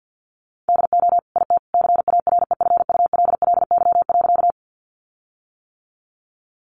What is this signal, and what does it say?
Morse code, heard as a keyed tone, about 48 wpm, 700 Hz. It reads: DO IT CAREFULLY.
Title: Sentences from Top 1300 Words 50wpm